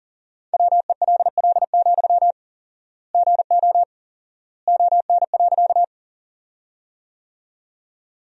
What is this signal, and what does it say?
Morse code, heard as a keyed tone, about 40 wpm, 700 Hz. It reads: WELL, GO ON.
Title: 4x Word-Spacing - Sentences from Top 200 Words - 40wpm